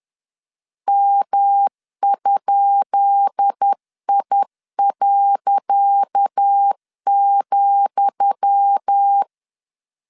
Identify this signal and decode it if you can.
Morse code, heard as a keyed tone, about 10 wpm, 800 Hz. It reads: M?I.,
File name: LEZIONI / 10 SIGNS / 10D COMA 10wpm 3013